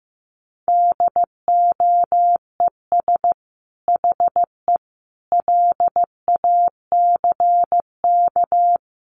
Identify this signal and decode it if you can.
Morse code, heard as a keyed tone, about 15 wpm, 700 Hz. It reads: DOES HE LACK
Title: Sentences from Top 1100 Words 15wpm